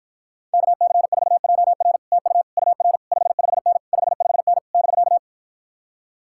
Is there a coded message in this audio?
Yes: Morse code from 0.5 to 5.2 s, heard as a keyed tone, about 52 wpm, 700 Hz. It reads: KC4JR TU UR 55N 55N <BK>